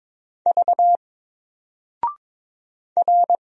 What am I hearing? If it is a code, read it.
Morse code, heard as a keyed tone, about 22 wpm, 700 Hz. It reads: V R
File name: Single-Letter-Number-22wpm